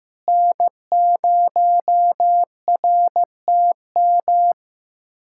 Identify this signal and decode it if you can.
Morse code, heard as a keyed tone, about 15 wpm, 700 Hz. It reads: N0RTM
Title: USA Callsign DE Callsign - 5 Characters - 15wpm